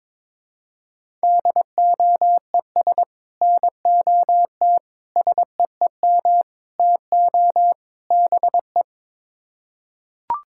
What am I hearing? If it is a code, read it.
Morse code, heard as a keyed tone, about 22 wpm, 700 Hz. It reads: DOES NOT SEEM TO BE
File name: practice / 22wpm / Build A Phrase - 5-Word Phrases - 22wpm